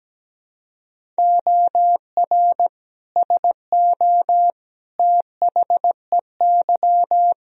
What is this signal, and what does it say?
Morse code, heard as a keyed tone, about 17 wpm, 700 Hz. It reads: OR SO THEY
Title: Sentences from Top 100 Words and Abbreviations 17wpm